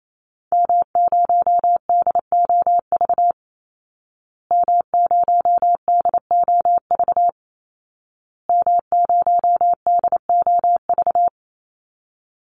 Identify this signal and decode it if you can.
Morse code, heard as a keyed tone, about 28 wpm, 700 Hz. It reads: M0DOV M0DOV M0DOV
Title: Call Signs Repeated 2x - 28wpm